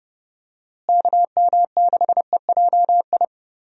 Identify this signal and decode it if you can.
Morse code, heard as a keyed tone, about 31 wpm, 700 Hz. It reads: KM6EJI
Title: USA Call Signs - 6 Characters - Repeated 3x - 30wpm